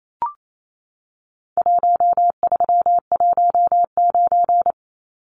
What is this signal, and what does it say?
Morse code, heard as a keyed tone, about 28 wpm, 700 Hz. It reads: 1319
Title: Numbers - Contest Style - 4-Digit 28wpm